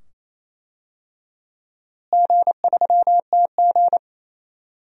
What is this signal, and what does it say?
Morse code, heard as a keyed tone, about 28 wpm, 700 Hz. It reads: G3TG
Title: DXCC-100 Call Signs - 4 Characters - Repeated 2x - 28wpm